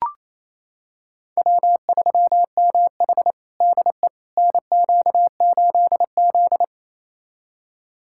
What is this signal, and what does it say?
Morse code, heard as a keyed tone, about 28 wpm, 700 Hz. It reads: W3MH DE NQ8Z
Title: USA Callsign DE Callsign - 4 Characters - 28wpm